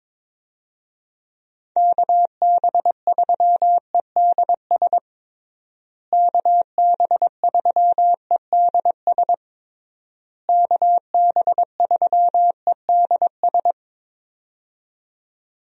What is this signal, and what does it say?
Morse code, heard as a keyed tone, about 22 wpm, 700 Hz. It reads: KB3EDS KB3EDS KB3EDS
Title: USA Call Signs - 6 Characters - Repeated 3x - 22wpm